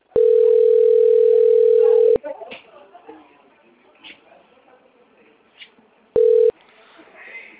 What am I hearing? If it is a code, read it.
Ringback tone (North America), 440 + 480 Hz, with speech.
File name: out- - - - - .8095